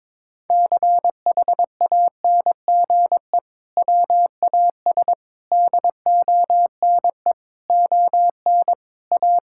Morse code, heard as a keyed tone, about 22 wpm, 700 Hz. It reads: CHANGE WAS DONE ON A
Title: Sentences from Top 1000 Words - Review Part 5 - 22wpm